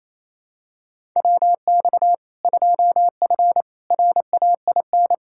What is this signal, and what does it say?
Morse code, heard as a keyed tone, about 27 wpm, 700 Hz. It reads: WX 2F RAIN
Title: K1USN SST QSO Practice 28wpm